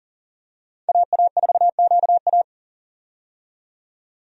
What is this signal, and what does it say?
Morse code, heard as a keyed tone, about 39 wpm, 700 Hz. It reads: AA4QA
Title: USA Call Signs - 5 Characters - 40wpm